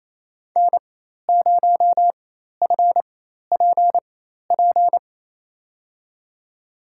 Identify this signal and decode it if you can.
Morse code, heard as a keyed tone, about 28 wpm, spaced as for 13 wpm, 700 Hz. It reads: N0FPP
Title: USA Call Signs - 5 Characters - 4x Character Spacing - 28wpm